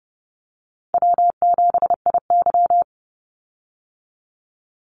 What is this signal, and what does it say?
Morse code, heard as a keyed tone, about 30 wpm, 700 Hz. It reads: W7IY
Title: Easy Call Signs Repeated 2x - 30wpm